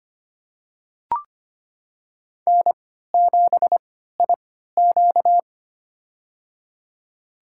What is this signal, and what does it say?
Morse code, heard as a keyed tone, about 25 wpm, spaced as for 14 wpm, 700 Hz. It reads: N7IQ
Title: USA Call Signs - 3x Character Spacing - 25wpm